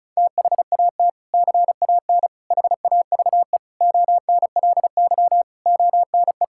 Morse code, heard as a keyed tone, about 36 wpm, 700 Hz. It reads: THAT CAN HAVE ONLY ONE
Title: Sentences from Top 1000 Words - Review Part 6 - 35wpm